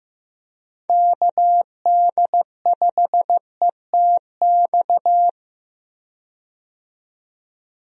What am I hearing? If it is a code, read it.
Morse code, heard as a keyed tone, about 15 wpm, 700 Hz. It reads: KD5ETX
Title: USA Call Signs - 6 Characters - 15wpm